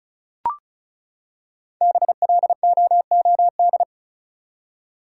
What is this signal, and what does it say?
Morse code, heard as a keyed tone, about 34 wpm, 700 Hz. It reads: BLOOD
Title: Five Letter Words - 35wpm